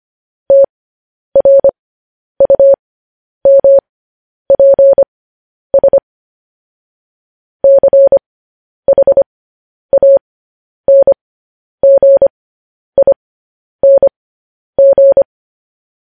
Morse code, heard as a keyed tone, about 25 wpm, spaced as for 10 wpm, 550 Hz. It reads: TRUMPS CHANGING
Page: News Headlines in Morse Code at 10 WPM